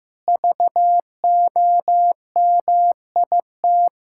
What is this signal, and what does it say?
Morse code, heard as a keyed tone, about 15 wpm, 700 Hz. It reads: VOMIT
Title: Pangram Challenge v2 - 15wpm